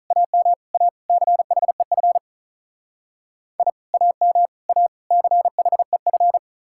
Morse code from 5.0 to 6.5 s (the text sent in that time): CHEF